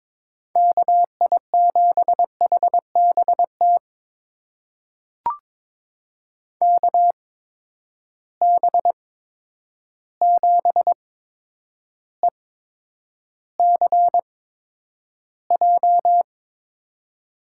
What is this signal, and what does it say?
Morse code, heard as a keyed tone, about 23 wpm, spaced as for 10 wpm, 700 Hz. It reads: KI7HBT KB7ECJ
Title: USA Call Signs - 6 Characters - 8x Character Spacing - 22wpm